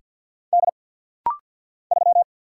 Morse code, heard as a keyed tone, about 50 wpm, 700 Hz. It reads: D 3
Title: Single Letter-Number - Rapid-Fire 50wpm